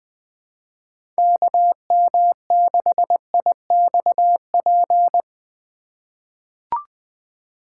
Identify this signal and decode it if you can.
Morse code, heard as a keyed tone, about 20 wpm, 700 Hz. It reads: KM6IXP
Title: USA Call Signs - 6 Characters - 4x Character Spacing - 20wpm